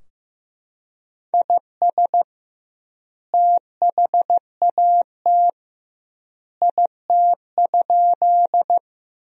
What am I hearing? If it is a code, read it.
Morse code, heard as a keyed tone, about 15 wpm, 700 Hz. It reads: IS THAT IT?
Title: 2x Word-Spacing - Sentences from Top 100 Words - 15wpm